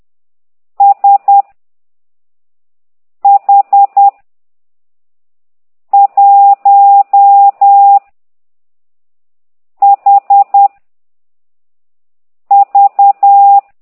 Morse code, heard as a keyed tone, about 10 wpm, 800 Hz. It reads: S H 1 H V
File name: ct2idw / exerc15